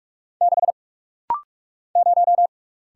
Morse code, heard as a keyed tone, about 44 wpm, 690 Hz. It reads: 6 0
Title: Single Letter-Number - Rapid-Fire 45wpm